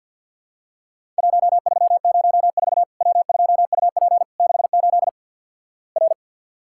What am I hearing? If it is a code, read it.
Morse code, heard as a keyed tone, about 50 wpm, 700 Hz. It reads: 1304 W2UP 68 R
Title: Worked All Europe DX Contest - QTC Exchange with Repeats - 50wpm